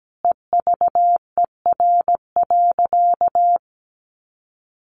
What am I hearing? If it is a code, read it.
Morse code, heard as a keyed tone, about 17 wpm, 700 Hz. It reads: EVER.
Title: Sentences from Top 1300 Words 17wpm